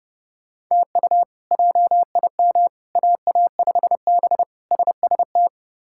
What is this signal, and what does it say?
Morse code, heard as a keyed tone, about 30 wpm, 700 Hz. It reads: TU JIM AA5B SST
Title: K1USN SST QSO Practice - 30wpm